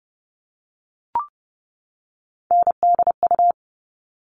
Morse code, heard as a keyed tone, about 30 wpm, 700 Hz. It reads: NDU
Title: All Cut Numbers - 3-Digit 30wpm